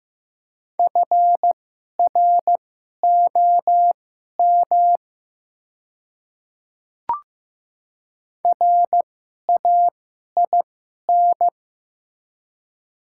Morse code, heard as a keyed tone, about 15 wpm, spaced as for 11 wpm, 700 Hz. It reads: FROM RAIN
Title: 2x Character Spacing - Top 500 Words - 15wpm